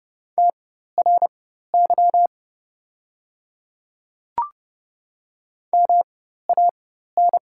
Morse code, heard as a keyed tone, about 30 wpm, spaced as for 14 wpm, 700 Hz. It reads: TRY MAN
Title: 4x Character Spacing - Top 200 Words - 30wpm